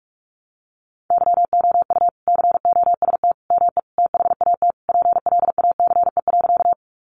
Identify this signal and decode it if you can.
Morse code, heard as a keyed tone, about 44 wpm, 700 Hz. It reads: YOU COST ME THAT PLACE.
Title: Sentences from Top 800 Words 45wpm